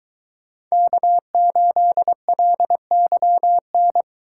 Morse code, heard as a keyed tone, about 22 wpm, 700 Hz. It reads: K8LYN